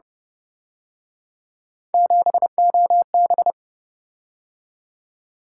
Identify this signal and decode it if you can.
Morse code, heard as a keyed tone, about 29 wpm, 700 Hz. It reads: 7OB
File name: Speed-Racing - Three Letter-Number Combinations - 25wpm